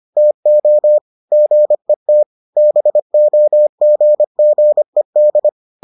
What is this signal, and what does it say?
Morse code, heard as a keyed tone, about 25 wpm, 600 Hz. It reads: TO GET BOGGED